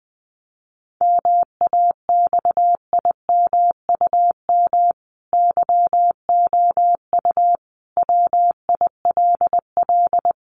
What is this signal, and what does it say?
Morse code, heard as a keyed tone, about 19 wpm, 700 Hz. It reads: MAXIMUM YOU WILL
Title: Sentences from Top 1300 Words 20wpm